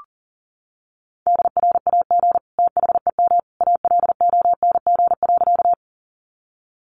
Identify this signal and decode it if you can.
Morse code, heard as a keyed tone, about 40 wpm, 700 Hz. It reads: DRAG THEM ALONG.
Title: Sentences from Top 1500 Words 40wpm